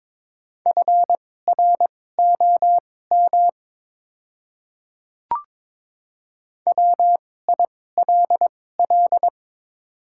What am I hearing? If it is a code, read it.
Morse code, heard as a keyed tone, about 22 wpm, spaced as for 16 wpm, 700 Hz. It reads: FROM WILL
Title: 2x Character Spacing - Top 100 Words - 22wpm